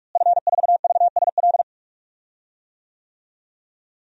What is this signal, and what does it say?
Morse code, heard as a keyed tone, about 47 wpm, 700 Hz. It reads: F4VSL
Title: DXCC-100 Call Signs - 5 Characters - Repeated 2x - 45wpm